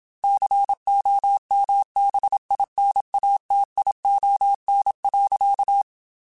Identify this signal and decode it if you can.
Morse code, heard as a keyed tone, about 27 wpm, 800 Hz. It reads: COMBINATION.
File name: Technical Subjects / Technical Subjects-mp3s / Sw Pwr Sup @25